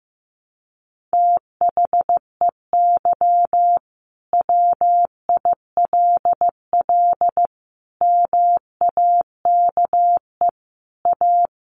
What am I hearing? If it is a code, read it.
Morse code, heard as a keyed tone, about 15 wpm, 700 Hz. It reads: THEY WILL MAKE A
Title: Sentences from Top 1500 Words 15wpm